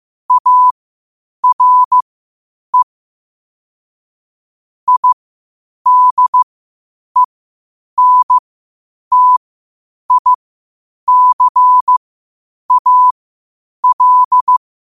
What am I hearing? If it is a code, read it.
Morse code, heard as a keyed tone, about 15 wpm, spaced as for 8 wpm, 1000 Hz. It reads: ARE IDENTICAL
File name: 8wpm